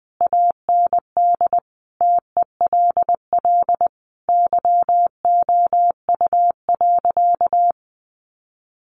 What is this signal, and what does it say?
Morse code, heard as a keyed tone, about 20 wpm, 700 Hz. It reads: AND TELL YOU.
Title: Sentences from Top 1100 Words 20wpm